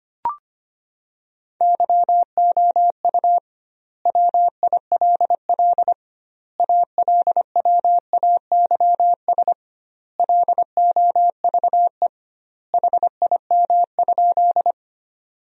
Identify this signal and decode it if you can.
Morse code, heard as a keyed tone, about 25 wpm, 700 Hz. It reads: YOU WILL ALWAYS LOVE HIM?
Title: 2x Word-Spacing - Sentences from Top 400 Words - 25wpm